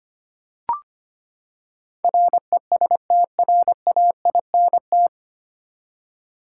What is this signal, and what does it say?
Morse code, heard as a keyed tone, about 25 wpm, 700 Hz. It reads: RESTRAINT
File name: Nine Letter Words - 25wpm